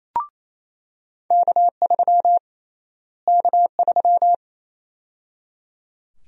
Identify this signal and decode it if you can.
Morse code, heard as a keyed tone, about 28 wpm, 700 Hz. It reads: K3 K3
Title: USA Call Sign Prefixes - Repeated 2x - 28wpm